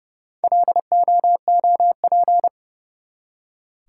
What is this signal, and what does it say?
Morse code, heard as a keyed tone, about 29 wpm, 700 Hz. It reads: LOOP
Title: Top 180 QSO Elements 30wpm